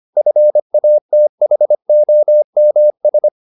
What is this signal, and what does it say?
Morse code, heard as a keyed tone, about 25 wpm, 600 Hz. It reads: FATHOMS
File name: the old man and the sea part 1 25wpm